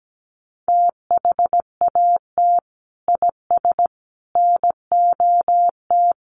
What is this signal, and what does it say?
Morse code, heard as a keyed tone, about 17 wpm, 700 Hz. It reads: THAT IS NOT
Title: Sentences from Top 600 Words 17wpm